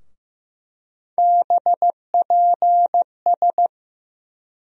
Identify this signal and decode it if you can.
Morse code, heard as a keyed tone, about 15 wpm, 700 Hz. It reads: BPS
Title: USA Call Sign Postfixes - 3 Letters 15wpm